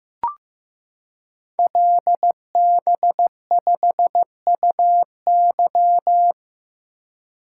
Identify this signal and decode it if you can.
Morse code, heard as a keyed tone, about 15 wpm, 700 Hz. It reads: LB5UY
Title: Code Groups - 5 Letter:Numbers - 15wpm